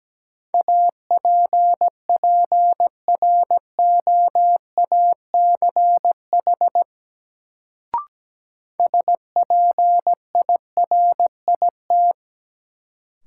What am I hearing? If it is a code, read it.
Morse code, heard as a keyed tone, about 17 wpm, 700 Hz. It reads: APPROACH SPIRIT
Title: Sentences from Top 1400 Words - New Word Drill Alt - 17wpm